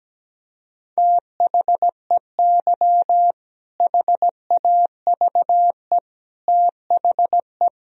Morse code, heard as a keyed tone, about 17 wpm, 700 Hz. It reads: THEY HAVE THE
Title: Sentences from Top 1000 Words - Review Part 3 - 17wpm